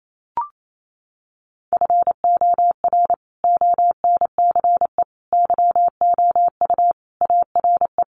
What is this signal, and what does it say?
Morse code, heard as a keyed tone, about 28 wpm, 700 Hz. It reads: FOR ONCE YOU ARE
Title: Sentences from Top 300 Words 28wpm